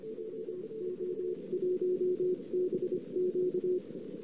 Morse code, heard as a keyed tone, about 26 wpm, spaced as for 23 wpm, 360 Hz. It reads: VO1BQ
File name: CW / 2008-ss-cw-k5tr-op-n5rz -14:00